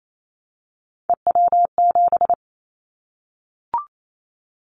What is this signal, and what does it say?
Morse code, heard as a keyed tone, about 28 wpm, 700 Hz. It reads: EW7
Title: Callsign Prefixes for Top 100 DXCC - 28wpm